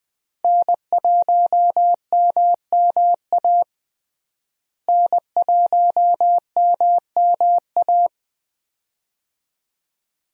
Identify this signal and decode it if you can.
Morse code, heard as a keyed tone, about 21 wpm, 700 Hz. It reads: N1MMA N1MMA
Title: USA Call Signs - 5 Characters - Repeated 2x - 20wpm